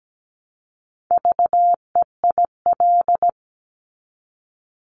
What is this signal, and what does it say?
Morse code, heard as a keyed tone, about 17 wpm, 700 Hz. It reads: VEIL
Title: Multi-letter Phonemes Repeated 3x 17wpm